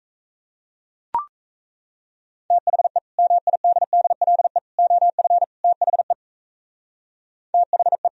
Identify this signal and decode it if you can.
Morse code, heard as a keyed tone, about 35 wpm, 700 Hz. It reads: THE MIDDLE OF THE THE
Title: Speed-Racing - Sets of 4 Words 28wpm